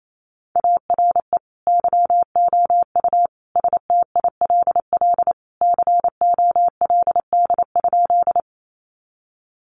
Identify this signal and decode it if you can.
Morse code, heard as a keyed tone, about 28 wpm, 700 Hz. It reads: ARE YOU STILL COLD?
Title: Sentences from Top 400 Words 28wpm